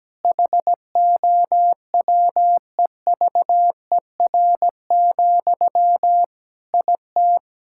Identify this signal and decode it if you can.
Morse code, heard as a keyed tone, about 17 wpm, 700 Hz. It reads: HOWEVER, IT
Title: Sentences from Top 1600 Words 17wpm